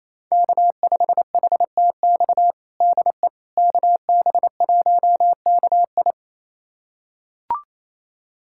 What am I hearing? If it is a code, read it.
Morse code, heard as a keyed tone, about 28 wpm, 700 Hz. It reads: K5HTX DE KB1KI
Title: USA Callsign DE Callsign - 5 Characters - 28wpm